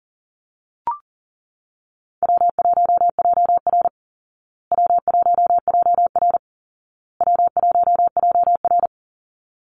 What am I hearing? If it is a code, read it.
Morse code, heard as a keyed tone, about 40 wpm, 700 Hz. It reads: W1JR W1JR W1JR
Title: Easy Call Signs Repeated 3x - 40wpm